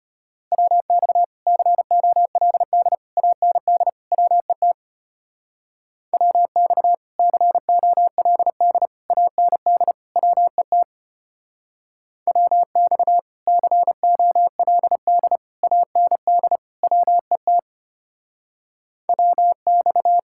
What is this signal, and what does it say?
Morse code, heard as a keyed tone, about 33 wpm, 700 Hz. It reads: WX COLD AND WET WX COLD AND WET WX COLD AND WET WX